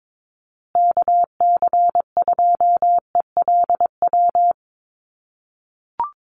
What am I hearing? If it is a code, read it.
Morse code, heard as a keyed tone, about 22 wpm, 700 Hz. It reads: KC2ELW
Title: USA Call Signs Repeated 2x 22wpm